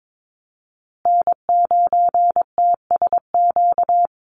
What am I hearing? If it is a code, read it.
Morse code, heard as a keyed tone, about 22 wpm, 700 Hz. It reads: N9TSQ
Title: USA Call Signs Repeated 2x 22wpm